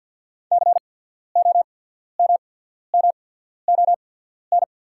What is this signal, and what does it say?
Morse code, heard as a keyed tone, about 48 wpm, spaced as for 13 wpm, 690 Hz. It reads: COMMON
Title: 8x Character Spacing - Top 500 Words - 50wpm